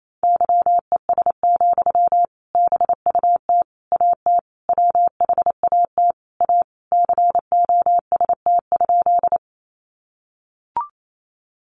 Morse code, heard as a keyed tone, about 28 wpm, 700 Hz. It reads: YES, BUT AT WHAT A COST?
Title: Sentences from Top 800 Words 28wpm